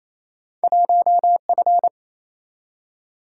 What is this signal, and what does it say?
Morse code, heard as a keyed tone, about 28 wpm, 700 Hz. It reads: 1F
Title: Single-Letter Single-Number 28wpm